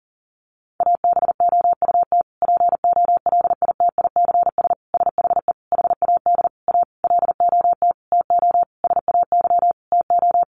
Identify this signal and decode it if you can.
Morse code, heard as a keyed tone, about 40 wpm, 700 Hz. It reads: ABOUT POLITICS SHE HAD A LOT TO SAY TO
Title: Sentences from Top 1000 Words 40wpm